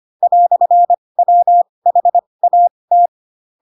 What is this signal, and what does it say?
Morse code, heard as a keyed tone, about 25 wpm, spaced as for 20 wpm, 700 Hz. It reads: "WHAT
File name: Pinocchio / Pinocchio ch2 15 25 ik1hge